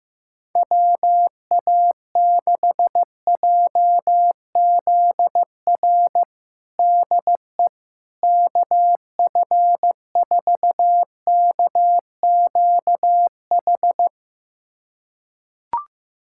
Morse code, heard as a keyed tone, about 15 wpm, 700 Hz. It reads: WA6JZR DE KF4KQH
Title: USA Callsign DE Callsign - 6 Characters - 15wpm